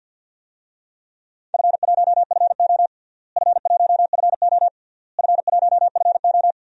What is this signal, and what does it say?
Morse code, heard as a keyed tone, about 48 wpm, 700 Hz. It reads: F1FO F1FO F1FO
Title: DXCC-100 Call Signs - 4 Characters - Repeated 3x - 50wpm